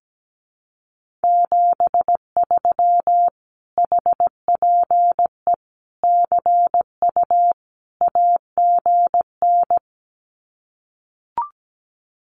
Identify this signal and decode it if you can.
Morse code, heard as a keyed tone, about 17 wpm, 700 Hz. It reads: 73 HPE CU AGN
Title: Common QSO Phrases Encore 17wpm